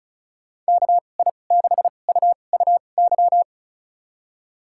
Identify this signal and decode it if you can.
Morse code, heard as a keyed tone, about 34 wpm, spaced as for 25 wpm, 700 Hz. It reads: KI6UUY
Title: USA Call Signs - 6 Characters - 2x Character Spacing - 35wpm